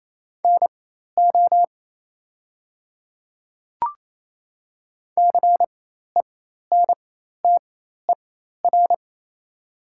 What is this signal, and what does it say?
Morse code, heard as a keyed tone, about 28 wpm, spaced as for 13 wpm, 700 Hz. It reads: NO CENTER
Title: 4x Character Spacing - Top 500 Words - 28wpm